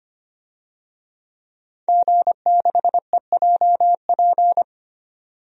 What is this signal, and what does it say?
Morse code, heard as a keyed tone, about 25 wpm, 700 Hz. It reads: G6EJP
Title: DXCC-100 Call Signs - 5 Characters - 25wpm